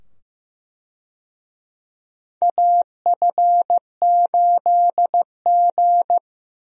Morse code, heard as a keyed tone, about 15 wpm, 700 Hz. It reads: AF8G
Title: USA Call Signs - 4 Characters - 15wpm